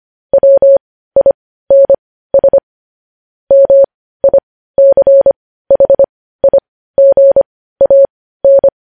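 Morse code, heard as a keyed tone, about 25 wpm, spaced as for 15 wpm, 550 Hz. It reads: WINS MICHIGAN
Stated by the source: News headlines sent as Morse practice at 15.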